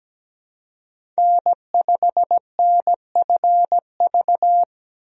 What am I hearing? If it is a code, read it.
Morse code, heard as a keyed tone, about 17 wpm, 700 Hz. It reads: N5NFV
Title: USA Call Signs - 5 Characters - 17wpm